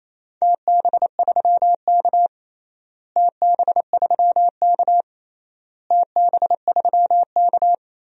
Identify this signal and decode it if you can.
Morse code, heard as a keyed tone, about 28 wpm, 700 Hz. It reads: TB3K TB3K TB3K
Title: DXCC-100 Call Signs - 4 Characters - Repeated 3x - 28wpm